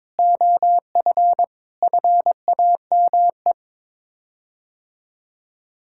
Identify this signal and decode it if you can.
Morse code, heard as a keyed tone, about 22 wpm, 700 Hz. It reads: OF FAME
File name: practice / 22wpm / Sets of 7 Words - 22wpm